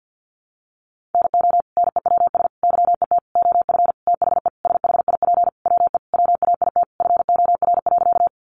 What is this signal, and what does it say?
Morse code, heard as a keyed tone, about 50 wpm, 700 Hz. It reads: NO NEWS YET OF THE SHIP WE WAIT FOR.
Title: Sentences from Top 800 Words 50wpm